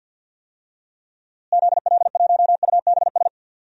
Morse code, heard as a keyed tone, about 51 wpm, 700 Hz. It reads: ZL1UBS
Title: DXCC-100 Call Signs - 6 Characters - Repeated 3x - 50wpm